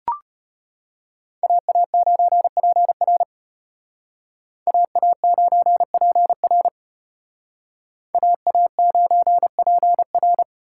Morse code, heard as a keyed tone, about 33 wpm, 700 Hz. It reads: AA9PR AA9PR AA9PR